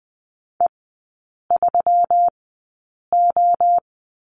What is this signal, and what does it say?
Morse code, heard as a keyed tone, about 20 wpm, spaced as for 8 wpm, 690 Hz. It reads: E3O
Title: 191 Lesson 31e - Letter v - Three Characters 20wpm